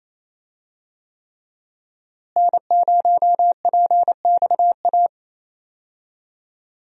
Morse code, heard as a keyed tone, about 27 wpm, 700 Hz. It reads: N0PXA
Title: USA Call Signs - 5 Characters - Repeated 3x - 28wpm